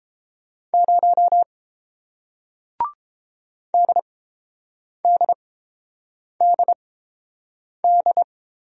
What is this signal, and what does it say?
Morse code, heard as a keyed tone, about 31 wpm, spaced as for 7 wpm, 700 Hz. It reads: 0 DDDD